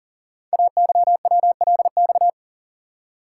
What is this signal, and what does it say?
Morse code, heard as a keyed tone, about 40 wpm, 700 Hz. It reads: AYWLX
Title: Code Groups - 5 Letters - 40wpm